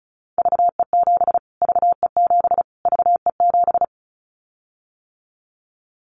Morse code, heard as a keyed tone, about 34 wpm, 700 Hz. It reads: VE7 VE7 VE7
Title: Callsign Prefixes for Top 25 DXCC - Repeated 3x - 35wpm